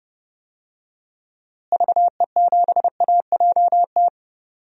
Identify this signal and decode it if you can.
Morse code, heard as a keyed tone, about 30 wpm, 700 Hz. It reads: VE7AJT
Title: DXCC-100 Call Signs - 6 Characters - Repeated 2x - 30wpm